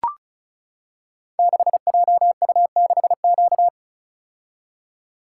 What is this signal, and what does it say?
Morse code, heard as a keyed tone, about 36 wpm, 700 Hz. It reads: 6JU6Q
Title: Code Groups - 5 Letter:Numbers - 35wpm